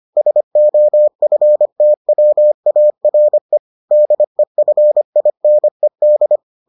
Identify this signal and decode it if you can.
Morse code, heard as a keyed tone, about 25 wpm, 600 Hz. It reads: SOFTWARE DEFINED